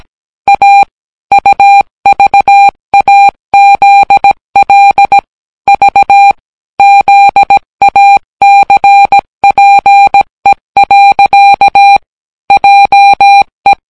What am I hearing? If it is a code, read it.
Morse code, heard as a keyed tone, about 18 wpm, 800 Hz. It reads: A UVAZL V ZACPE. JE